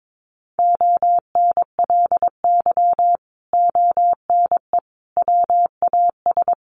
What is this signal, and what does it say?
Morse code, heard as a keyed tone, about 22 wpm, 700 Hz. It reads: ONLY ONE WAS
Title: Sentences from Top 1400 Words 22wpm